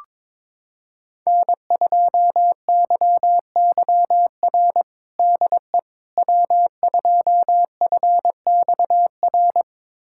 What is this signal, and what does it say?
Morse code, heard as a keyed tone, about 22 wpm, 700 Hz. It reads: N2YYR DE W2FXR
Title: USA Callsign DE Callsign - 5 Characters - 22wpm